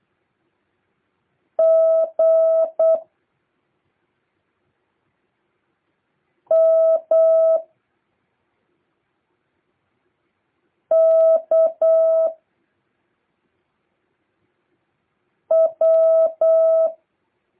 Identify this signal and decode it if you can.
Morse code, heard as a keyed tone, about 8 wpm, 650 Hz. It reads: G M K W
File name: jh8cbh / 2023cw